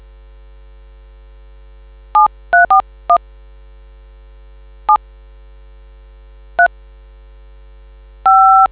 DTMF, heard as telephone touch tones, tones of about 110 ms. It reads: 7341 7 3 5